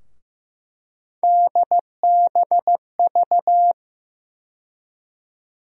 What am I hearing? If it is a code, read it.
Morse code, heard as a keyed tone, about 15 wpm, 700 Hz. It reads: DBV
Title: USA Call Sign Postfixes - 3 Letters 15wpm